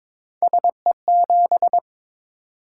Morse code, heard as a keyed tone, about 23 wpm, 700 Hz. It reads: SE7
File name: practice / 22wpm / Three Letter-Number Combinations - 22wpm